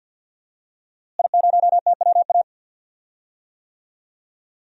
Morse code, heard as a keyed tone, about 51 wpm, 700 Hz. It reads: I0TWA
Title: DXCC-100 Call Signs - 5 Characters - Repeated 3x - 50wpm